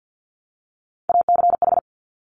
Morse code, heard as a keyed tone, about 50 wpm, 700 Hz. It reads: ACH
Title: Top Three Letter Combinations in Words 50wpm